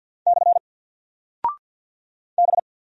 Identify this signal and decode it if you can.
Morse code, heard as a keyed tone, about 48 wpm, 700 Hz. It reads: / B
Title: Single Character (Letter:Number:Punctuation:Prosign) 50wpm